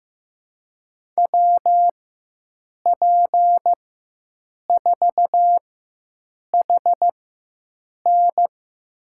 Morse code, heard as a keyed tone, about 15 wpm, spaced as for 7 wpm, 700 Hz. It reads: WP4HN
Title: USA Call Signs - 5 Characters - 4x Character Spacing - 15wpm